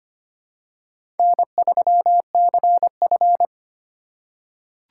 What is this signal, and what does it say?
Morse code, heard as a keyed tone, about 25 wpm, 700 Hz. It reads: N3CF
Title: USA Call Signs - 4 Characters - 2x Character Spacing - 25wpm